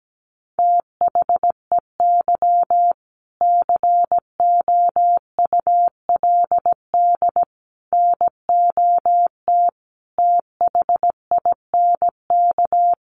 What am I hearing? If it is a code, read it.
Morse code, heard as a keyed tone, about 17 wpm, 700 Hz. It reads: THEY COULD NOT THINK
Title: Sentences from Top 1000 Words 17wpm